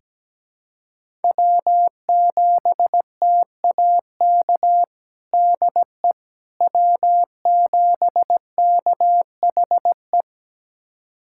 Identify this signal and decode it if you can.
Morse code, heard as a keyed tone, about 17 wpm, 700 Hz. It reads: W7TAK DE W7KHE
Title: USA Callsign DE Callsign - 5 Characters - 17wpm